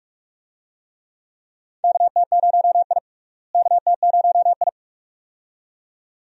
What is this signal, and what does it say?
Morse code, heard as a keyed tone, about 42 wpm, 700 Hz. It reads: KT0I KT0I
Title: USA Call Signs - 4 Characters - Repeated 2x - 45wpm